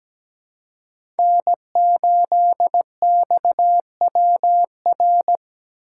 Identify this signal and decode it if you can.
Morse code, heard as a keyed tone, about 17 wpm, 700 Hz. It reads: N8XWR
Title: USA Call Signs - 2x Character Spacing - 17wpm